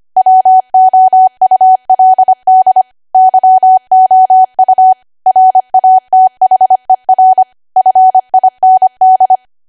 Morse code, heard as a keyed tone, about 25 wpm, 750 Hz. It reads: WOULD YOU RATHER FIND
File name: Archive / 25 WPM / 25WPM